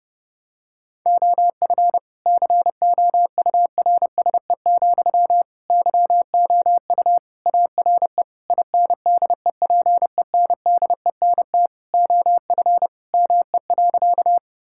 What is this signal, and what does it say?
Morse code, heard as a keyed tone, about 32 wpm, 700 Hz. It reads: OF COURSE, YOU ARE INDEPENDENT OF ME.
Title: Sentences from Top 1000 Words - Review Part 6 - 30wpm